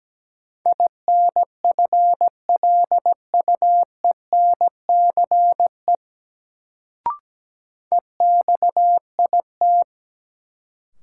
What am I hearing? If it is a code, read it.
Morse code, heard as a keyed tone, about 17 wpm, 700 Hz. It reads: INFLUENCE EXIT
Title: Sentences from Top 1100 Words - New Word Drill - 17wpm